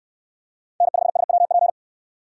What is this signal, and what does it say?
Morse code, heard as a keyed tone, about 69 wpm, 700 Hz. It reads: N5SWP